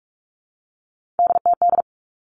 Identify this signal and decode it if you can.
Morse code, heard as a keyed tone, about 46 wpm, 700 Hz. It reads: DTD
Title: Three Challenging Letter-Number Combinations 45wpm